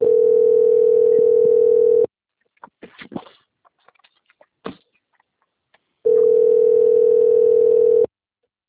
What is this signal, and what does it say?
Ringback tone (North America), 440 + 480 Hz.
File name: out- - - - - .37491